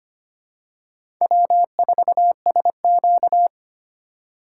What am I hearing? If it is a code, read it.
Morse code, heard as a keyed tone, about 25 wpm, 700 Hz. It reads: W4SQ
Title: USA Call Signs - 4 Characters - 4x Character Spacing - 25wpm